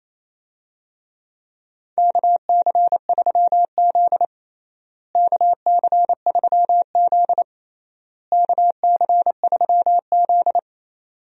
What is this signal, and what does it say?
Morse code, heard as a keyed tone, about 27 wpm, 700 Hz. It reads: KC3Z KC3Z KC3Z
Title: USA Call Signs - 4 Characters - Repeated 3x - 28wpm